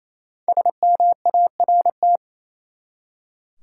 Morse code, heard as a keyed tone, about 28 wpm, 700 Hz. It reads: SMART
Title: Five Letter Words - 28wpm